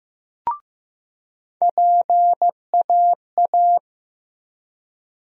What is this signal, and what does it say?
Morse code, heard as a keyed tone, about 15 wpm, 700 Hz. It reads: PAA
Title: USA Call Sign Postfixes - 3 Letters 15wpm